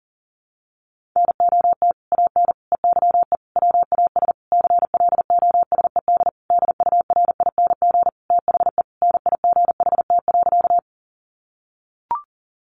Morse code, heard as a keyed tone, about 40 wpm, 700 Hz. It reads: NOT AN EYE WAS CLOSED DURING THE NIGHT.
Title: Sentences from Top 1400 Words 40wpm